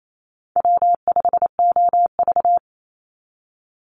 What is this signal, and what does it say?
Morse code, heard as a keyed tone, about 28 wpm, 700 Hz. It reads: W5OV
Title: Call Signs - 28wpm